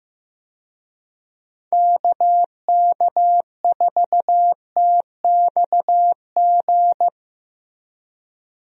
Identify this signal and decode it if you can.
Morse code, heard as a keyed tone, about 15 wpm, 700 Hz. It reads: KK4TXG
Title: USA Call Signs - 6 Characters - Repeated 2x - 15wpm